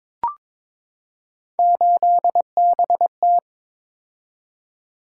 Morse code, heard as a keyed tone, about 21 wpm, 700 Hz. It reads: 8BT
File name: practice / 22wpm / Three Letter-Number Combinations - 22wpm